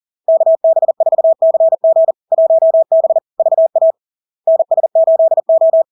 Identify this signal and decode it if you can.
Morse code, heard as a keyed tone, about 40 wpm, 650 Hz. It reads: KB4CG 1B VA NS8O
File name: LARK-FD-2022-Log-in-CW-40wpm-650-Hz